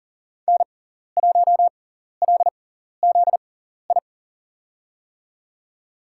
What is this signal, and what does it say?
Morse code, heard as a keyed tone, about 40 wpm, spaced as for 14 wpm, 700 Hz. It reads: N1LZI
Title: USA Call Signs - 5 Characters - 6x Character Spacing - 40wpm